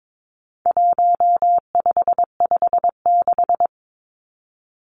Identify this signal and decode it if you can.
Morse code, heard as a keyed tone, about 22 wpm, 700 Hz. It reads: 1556
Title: Numbers - 4-Digit 22wpm